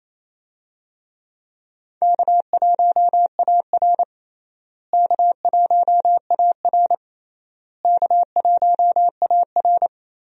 Morse code, heard as a keyed tone, about 27 wpm, 700 Hz. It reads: K1AR K1AR K1AR
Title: USA Call Signs - 4 Characters - Repeated 3x - 28wpm